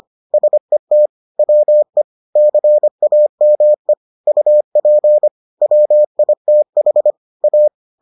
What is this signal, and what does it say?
Morse code, heard as a keyed tone, about 25 wpm, 600 Hz. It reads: SET WE CAME UP WITH A